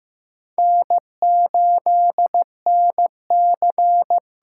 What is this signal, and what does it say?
Morse code, heard as a keyed tone, about 15 wpm, 700 Hz. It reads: N8NC
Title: USA Call Signs - 8x Character Spacing - 15wpm